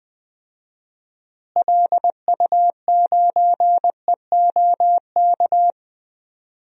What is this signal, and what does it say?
Morse code, heard as a keyed tone, about 20 wpm, 700 Hz. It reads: LU9EOK
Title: DXCC-100 Call Signs - 6 Characters - Repeated 2x - 20wpm